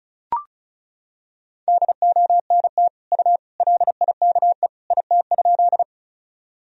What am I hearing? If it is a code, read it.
Morse code, heard as a keyed tone, about 34 wpm, 700 Hz. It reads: DONT U LIKE IT?
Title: Sentences from Top 300 Words and Abbreviations 35wpm